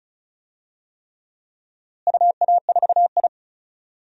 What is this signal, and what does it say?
Morse code, heard as a keyed tone, about 34 wpm, 700 Hz. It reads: UA4I
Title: DXCC-100 Call Signs - 4 Characters - Repeated 3x - 35wpm